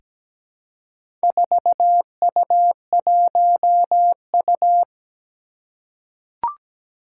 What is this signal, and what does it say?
Morse code, heard as a keyed tone, about 17 wpm, 700 Hz. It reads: 4U1U
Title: Four Challenging Letter-Number Combinations 17wpm